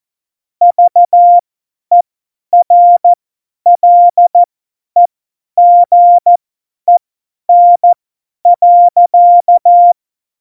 Morse code, heard as a keyed tone, about 14 wpm, spaced as for 10 wpm, 700 Hz. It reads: VERLEGEN.
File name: Morsekurs / Witze / Witz 06. Oktober 2020, 50 F70